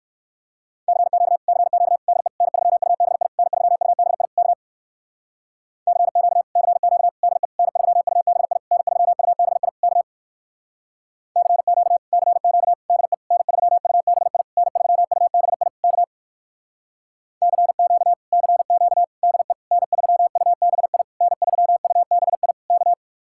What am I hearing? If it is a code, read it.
Morse code, heard as a keyed tone, about 58 wpm, 700 Hz. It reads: CQ CQ DE N3UBI N3UBI K CQ CQ DE N3UBI N3UBI K CQ CQ DE N3UBI N3UBI K CQ CQ DE N3UBI N3UBI K